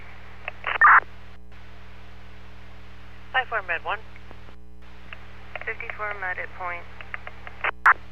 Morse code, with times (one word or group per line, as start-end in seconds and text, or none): none